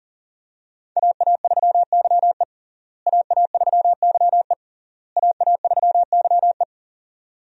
Morse code, heard as a keyed tone, about 40 wpm, 700 Hz. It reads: AA3YE AA3YE AA3YE
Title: USA Call Signs - 5 Characters - Repeated 3x - 40wpm